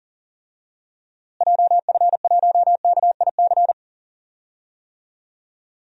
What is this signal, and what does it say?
Morse code, heard as a keyed tone, about 40 wpm, 700 Hz. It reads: JF1KIC
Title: DXCC-100 Call Signs - 6 Characters - Repeated 3x - 40wpm